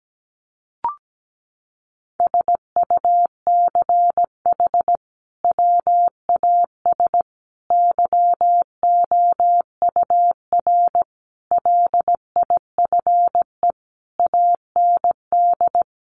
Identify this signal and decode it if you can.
Morse code, heard as a keyed tone, about 17 wpm, 700 Hz. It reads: SUCH WAS YOUR LIFE AND
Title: Sentences from Top 1500 Words 17wpm